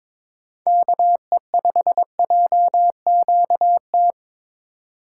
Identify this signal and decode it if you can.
Morse code, heard as a keyed tone, about 22 wpm, 700 Hz. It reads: KE5JQT
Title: USA Call Signs - 6 Characters - 4x Character Spacing - 22wpm